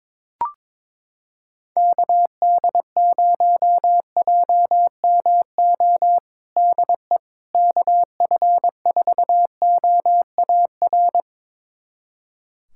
Morse code, heard as a keyed tone, about 22 wpm, 700 Hz. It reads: KD0JMO DE KF4OAR
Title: USA Callsign DE Callsign - 6 Characters - 22wpm